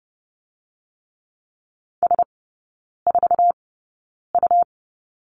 Morse code, heard as a keyed tone, about 30 wpm, spaced as for 9 wpm, 690 Hz. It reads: S4U
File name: Lesson 17e - Letter u - Three Characters 30wpm